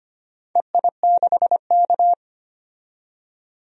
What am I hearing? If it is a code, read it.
Morse code, heard as a keyed tone, about 25 wpm, 700 Hz. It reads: EI6K
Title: DXCC-100 Call Signs - 4 Characters - 25wpm